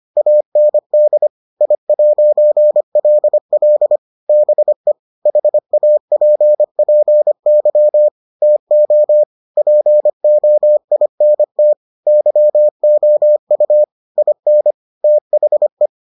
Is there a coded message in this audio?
Yes: Morse code from 0.2 to 15.9 s, heard as a keyed tone, about 25 wpm, 600 Hz. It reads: AND I'LL BE HAPPY TO POINT YOU IN THE